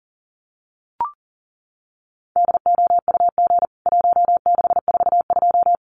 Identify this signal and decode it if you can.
Morse code, heard as a keyed tone, about 40 wpm, 700 Hz. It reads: DOUG 1642
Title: CWT Exchange 40wpm